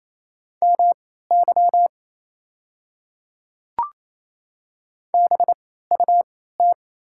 Morse code, heard as a keyed tone, about 28 wpm, spaced as for 16 wpm, 690 Hz. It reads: MY BUT
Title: 3x Character Spacing - Top 100 Words - 28wpm